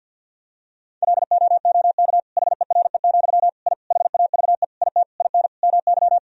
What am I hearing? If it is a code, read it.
Morse code, heard as a keyed tone, about 48 wpm, 700 Hz. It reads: LOOK HERE, I HAVE IT IN MY
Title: Sentences from Top 1000 Words - Review Part 1 - 50wpm